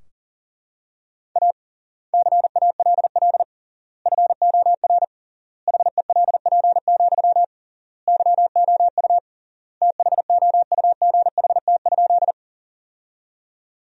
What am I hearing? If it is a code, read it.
Morse code, heard as a keyed tone, about 37 wpm, 700 Hz. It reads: A CALL FOR HELP, YOU THOUGHT?
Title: 3x Word-Spacing - Sentences from Top 300 Words - 40wpm